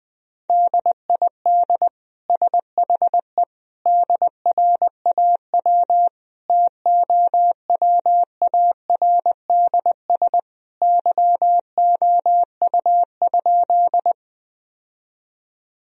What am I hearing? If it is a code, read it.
Morse code, heard as a keyed tone, about 20 wpm, 700 Hz. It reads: DID SHE DRAW TOWARDS YOU?
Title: Sentences from Top 1000 Words - Review Part 2 - 20wpm